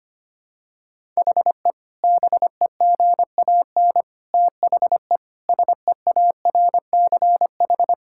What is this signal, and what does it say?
Morse code, heard as a keyed tone, about 25 wpm, 700 Hz. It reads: HE BEGAN THE SEARCH
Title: Sentences from Top 1000 Words - Review Part 5 - 25wpm